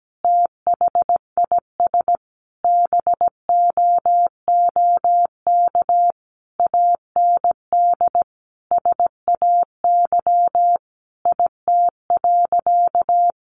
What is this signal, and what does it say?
Morse code, heard as a keyed tone, about 17 wpm, 700 Hz. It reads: THIS BOOK AND SAY IT.
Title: Sentences from Top 400 Words 17wpm